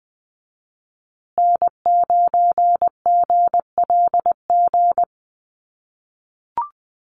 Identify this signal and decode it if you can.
Morse code, heard as a keyed tone, about 20 wpm, 700 Hz. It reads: N9GLG
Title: USA Call Signs Repeated 2x 20wpm